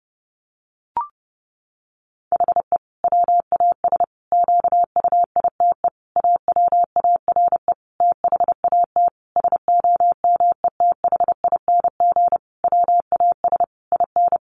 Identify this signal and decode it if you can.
Morse code, heard as a keyed tone, about 30 wpm, 700 Hz. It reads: HE WAS QUITE AWARE THAT SOMETHING WAS IN
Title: Sentences from Top 1400 Words 30wpm